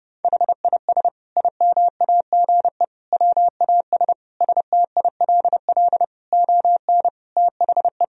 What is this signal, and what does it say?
Morse code, heard as a keyed tone, about 30 wpm, 700 Hz. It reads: HIS IMAGE WAS STILL ON THE
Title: Sentences from Top 1000 Words - Review Part 4 - 30wpm